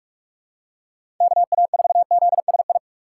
Morse code, heard as a keyed tone, about 44 wpm, 700 Hz. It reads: KA4ZSI
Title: USA Call Signs - 6 Characters - 2x Character Spacing - 45wpm